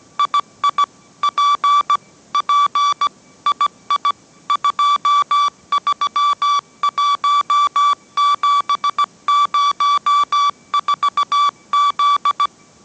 Morse code, heard as a keyed tone, about 17 wpm, 1160 Hz. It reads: II P P II 231704Z